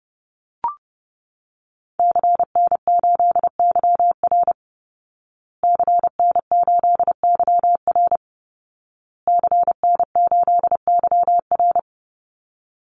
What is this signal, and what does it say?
Morse code, heard as a keyed tone, about 30 wpm, 700 Hz. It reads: CN8YR CN8YR CN8YR
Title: Call Signs Repeated 2x - 30wpm